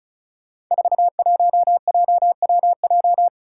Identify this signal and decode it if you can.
Morse code, heard as a keyed tone, about 36 wpm, 700 Hz. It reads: 41JWJ
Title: Five Challenging Letter-Number Combinations 35wpm